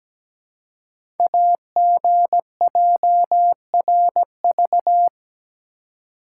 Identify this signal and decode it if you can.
Morse code, heard as a keyed tone, about 17 wpm, 700 Hz. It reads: AGJRV
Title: Code Groups - 5 Letter:Numbers - 17wpm